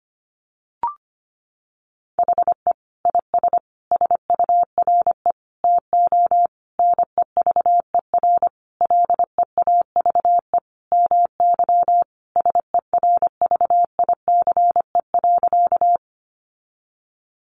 Morse code, heard as a keyed tone, about 25 wpm, 700 Hz. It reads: HE IS SURE TO NEVER LEAVE MY SERVICE.
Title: Sentences from Top 700 Words 25wpm